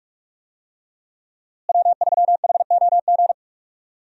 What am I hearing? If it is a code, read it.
Morse code, heard as a keyed tone, about 46 wpm, 700 Hz. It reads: W3HOG
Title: USA Call Signs - 5 Characters - Repeated 3x - 45wpm